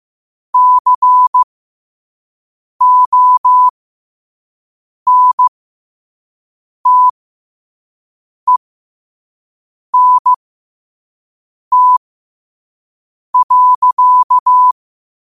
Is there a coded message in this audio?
Yes: Morse code from 0.5 to 14.7 s, heard as a keyed tone, about 15 wpm, spaced as for 5 wpm, 1000 Hz. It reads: CONTENT.